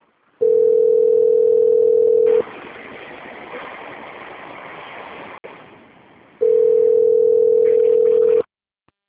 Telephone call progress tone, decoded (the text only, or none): ringback tone